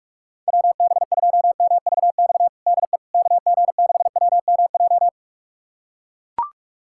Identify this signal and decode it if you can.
Morse code, heard as a keyed tone, about 42 wpm, 690 Hz. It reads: WB2MVX DE KG6WMJ
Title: USA Callsign DE Callsign - 6 Characters - 45wpm